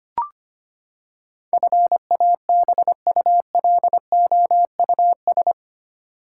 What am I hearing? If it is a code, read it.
Morse code, heard as a keyed tone, about 25 wpm, 700 Hz. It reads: FABULOUS
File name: Eight Letter Words - 25wpm